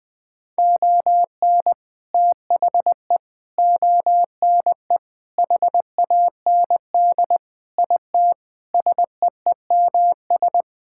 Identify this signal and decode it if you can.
Morse code, heard as a keyed tone, about 20 wpm, 700 Hz. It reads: ON THE ONE HAND IT SEEMS
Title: Sentences with 4-Word Phrases - 20wpm